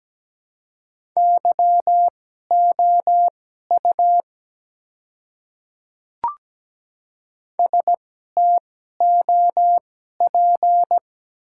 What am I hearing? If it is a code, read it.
Morse code, heard as a keyed tone, about 17 wpm, spaced as for 12 wpm, 690 Hz. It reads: YOU STOP
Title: 2x Character Spacing - Top 300 Words - 17wpm